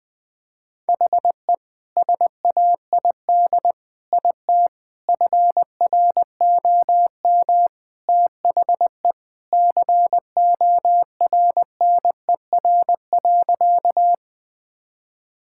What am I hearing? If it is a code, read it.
Morse code, heard as a keyed tone, about 20 wpm, 700 Hz. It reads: HE SAID IT FROM THE CORNER.
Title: Sentences from Top 1600 Words 20wpm